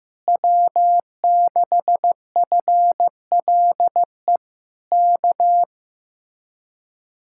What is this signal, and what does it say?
Morse code, heard as a keyed tone, about 15 wpm, 700 Hz. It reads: W6FLE K
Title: Speed-Racing Common QSO Phrases Encore - 15wpm